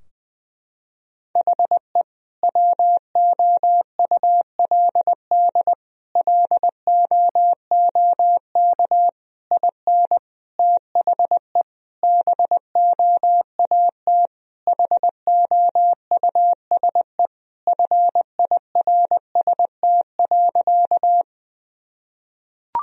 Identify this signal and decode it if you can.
Morse code, heard as a keyed tone, about 20 wpm, 700 Hz. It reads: HE WOULD LOOK IN THE BOAT HOUSE FIRST.
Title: Sentences from Top 1000 Words - Review Part 3 - 20wpm